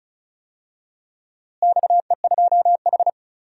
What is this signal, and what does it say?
Morse code, heard as a keyed tone, about 34 wpm, 700 Hz. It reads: XE2H
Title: DXCC-100 Call Signs - 4 Characters - Repeated 2x - 35wpm